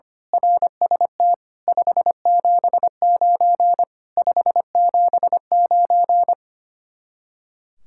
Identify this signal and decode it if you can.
Morse code, heard as a keyed tone, about 24 wpm, 700 Hz. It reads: RST 579 579
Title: Speed-Racing Common QSO Phrases - 25wpm